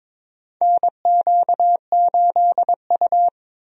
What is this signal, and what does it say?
Morse code, heard as a keyed tone, about 21 wpm, 700 Hz. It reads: NQ8U
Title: USA Call Signs - 4 Characters - 22wpm